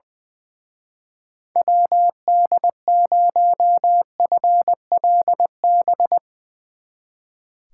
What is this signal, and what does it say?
Morse code, heard as a keyed tone, about 22 wpm, 700 Hz. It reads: WD0FLB
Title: Speed Racing - USA 2x3 Call Signs 20wpm